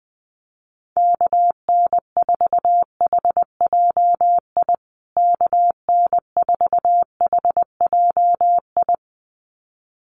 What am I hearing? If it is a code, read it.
Morse code, heard as a keyed tone, about 20 wpm, 700 Hz. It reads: KN4HJI KN4HJI
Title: USA Call Signs Repeated 2x 20wpm